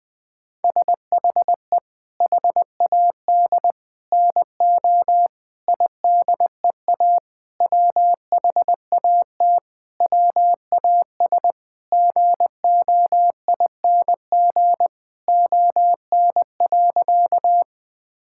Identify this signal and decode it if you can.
Morse code, heard as a keyed tone, about 20 wpm, 700 Hz. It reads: SHE HAD NO IDEA WHAT WAS GOING ON.
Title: Sentences with 7-Word Phrases - Spoken Beginning - 20wpm